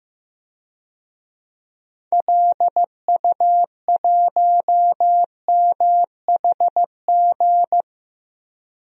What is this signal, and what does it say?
Morse code, heard as a keyed tone, about 16 wpm, 700 Hz. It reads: LU1MHG
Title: DXCC-100 Call Signs - 6 Characters - 15wpm